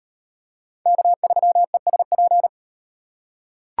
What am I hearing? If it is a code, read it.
Morse code, heard as a keyed tone, about 36 wpm, 700 Hz. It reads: K3ESP